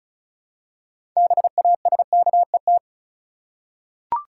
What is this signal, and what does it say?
Morse code, heard as a keyed tone, about 34 wpm, 700 Hz. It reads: BASKET
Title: Six Letter Words - 35wpm